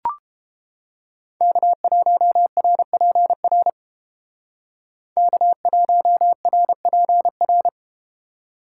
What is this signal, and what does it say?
Morse code, heard as a keyed tone, about 32 wpm, 700 Hz. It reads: K1RPR K1RPR